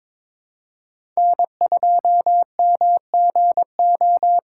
Morse code, heard as a keyed tone, about 22 wpm, 700 Hz. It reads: N2MGO
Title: USA Call Signs - 4x Character Spacing - 22wpm